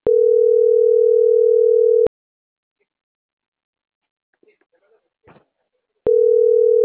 Ringback tone (North America), 440 + 480 Hz.